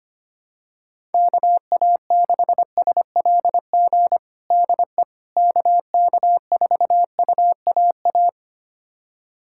Morse code, heard as a keyed tone, about 25 wpm, 700 Hz. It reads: KA6SLG DE KK4UAA
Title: USA Callsign DE Callsign - 6 Characters - 25wpm